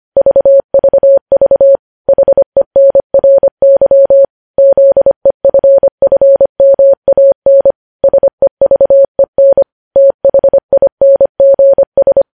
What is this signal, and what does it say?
Morse code, heard as a keyed tone, about 25 wpm, 550 Hz. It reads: VVV HENRY ZEFFMAN SEVEN THINGS
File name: podcast / News headlines-1x-25x25